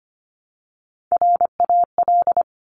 Morse code, heard as a keyed tone, about 25 wpm, 700 Hz. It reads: RAL
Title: Top Three Letter Combinations in Words 25wpm